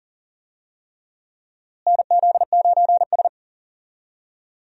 Morse code, heard as a keyed tone, about 40 wpm, 700 Hz. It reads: NZ9S
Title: USA Call Signs - 4 Characters - Repeated 3x - 40wpm